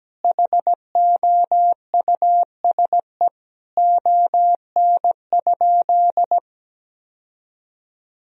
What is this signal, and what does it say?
Morse code, heard as a keyed tone, about 17 wpm, 700 Hz. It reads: HOUSE ON?
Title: Sentences from Top 1000 Words - Review Part 4 - 17wpm